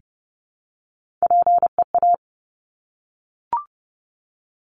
Morse code, heard as a keyed tone, about 30 wpm, 700 Hz. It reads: PEA
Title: Top Three Letter Combinations in Words 30wpm